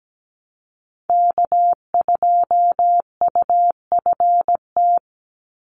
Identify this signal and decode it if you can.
Morse code, heard as a keyed tone, about 17 wpm, 700 Hz. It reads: K2UFT
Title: Call Signs Repeated 3x - 17wpm